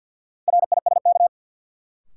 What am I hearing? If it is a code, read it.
Morse code, heard as a keyed tone, about 52 wpm, 700 Hz. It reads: RISK
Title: Sentences from Top 600 Words - New Word Drill - 50wpm